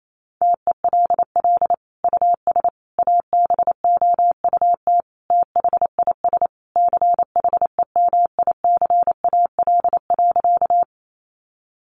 Morse code, heard as a keyed tone, about 28 wpm, 700 Hz. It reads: TELL US ABOUT THIS CHEMICAL.
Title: Sentences from Top 1200 Words 28wpm